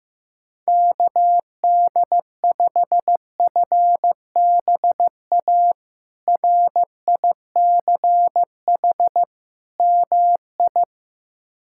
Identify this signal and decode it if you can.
Morse code, heard as a keyed tone, about 15 wpm, 700 Hz. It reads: KD5FBA RICH MI
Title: K1USN SST QSO Practice - 15wpm